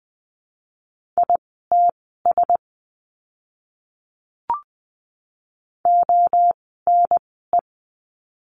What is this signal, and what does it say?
Morse code, heard as a keyed tone, about 20 wpm, spaced as for 15 wpm, 700 Hz. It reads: ITS ONE
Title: 054 Lesson 10f - Letter r - Words 20wpm